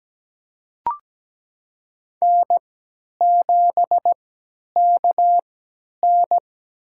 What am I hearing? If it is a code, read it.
Morse code, heard as a keyed tone, about 17 wpm, spaced as for 10 wpm, 700 Hz. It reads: N7KN
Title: USA Call Signs - 4 Characters - 3x Character Spacing - 17wpm